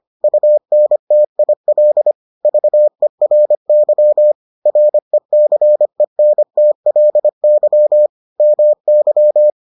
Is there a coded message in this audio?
Yes: Morse code from 0.2 to 9.5 s, heard as a keyed tone, about 25 wpm, 600 Hz. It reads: UNTIL VERY RECENTLY MY